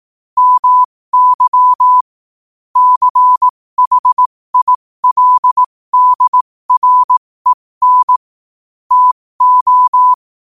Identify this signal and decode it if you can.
Morse code, heard as a keyed tone, about 18 wpm, spaced as for 15 wpm, 1000 Hz. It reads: MY CHILDREN TO
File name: qotdfiles / 15wpm